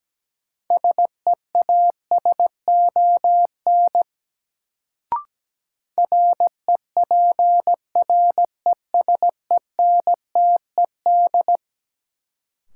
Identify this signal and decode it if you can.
Morse code, heard as a keyed tone, about 17 wpm, 700 Hz. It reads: SEASON REPRESENTED
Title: Sets of 5 Words - New Word Drill - 17wpm